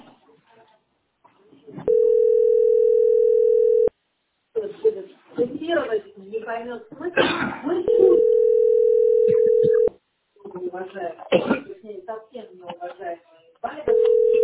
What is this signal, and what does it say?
Ringback tone (North America), 440 + 480 Hz, with speech.
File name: 3ddc-4cb5-87e9